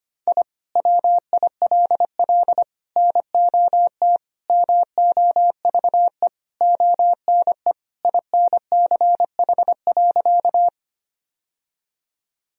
Morse code, heard as a keyed tone, about 25 wpm, 700 Hz. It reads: I WILL NOT MOVE ONE INCH.
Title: Sentences from Top 1000 Words - Review Part 3 - 25wpm